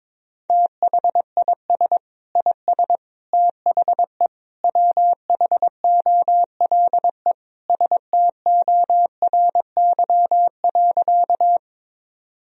Morse code, heard as a keyed tone, about 22 wpm, 700 Hz. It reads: THIS IS THE WHOLE STORY.
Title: Sentences from Top 1000 Words - Review Part 1 - 22wpm